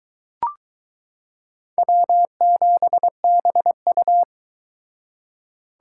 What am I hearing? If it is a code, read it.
Morse code, heard as a keyed tone, about 22 wpm, 700 Hz. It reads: W7BU